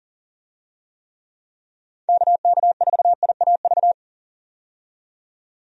Morse code, heard as a keyed tone, about 40 wpm, 700 Hz. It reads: KK4IAV
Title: USA Call Signs - 6 Characters - 40wpm